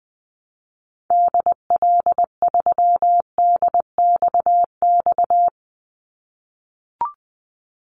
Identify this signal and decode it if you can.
Morse code, heard as a keyed tone, about 20 wpm, 700 Hz. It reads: DL3DXX
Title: Call Signs - 20wpm